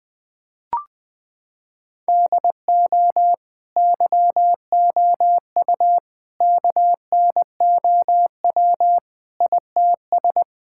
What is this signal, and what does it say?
Morse code, heard as a keyed tone, about 20 wpm, 700 Hz. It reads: DO YOU KNOW ITS
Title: Sentences from Top 1000 Words - Review Part 5 - 20wpm